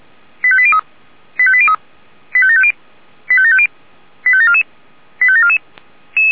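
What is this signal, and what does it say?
Five-tone selective call (ZVEI), heard as quick runs of five tones, tones of about 70 ms. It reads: 85803 85803 85750 85750 85744 85744
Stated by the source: ZVEI five-tone selective call 85730 received by SDR.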